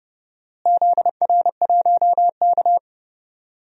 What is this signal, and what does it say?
Morse code, heard as a keyed tone, about 31 wpm, 700 Hz. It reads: ZR1K
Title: DXCC-100 Call Signs - 4 Characters - Repeated 2x - 30wpm